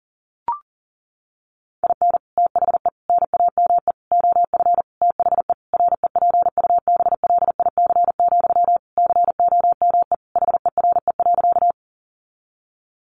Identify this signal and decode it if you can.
Morse code, heard as a keyed tone, about 40 wpm, 700 Hz. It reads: IN THE NAME OF THE REPUBLIC, COME HERE.
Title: Sentences from Top 1500 Words 40wpm